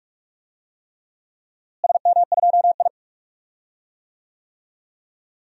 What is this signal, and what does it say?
Morse code, heard as a keyed tone, about 46 wpm, 700 Hz. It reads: SM2I
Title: DXCC-100 Call Signs - 4 Characters - Repeated 3x - 45wpm